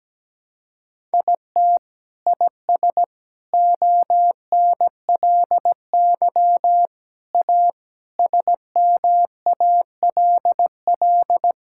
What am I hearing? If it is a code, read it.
Morse code, heard as a keyed tone, about 17 wpm, 700 Hz. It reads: IT IS ONLY A SMALL
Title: Sentences from Top 1000 Words - Review Part 5 - 17wpm